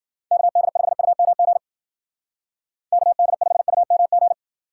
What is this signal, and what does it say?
Morse code, heard as a keyed tone, about 52 wpm, 700 Hz. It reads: KD5UMG KD5UMG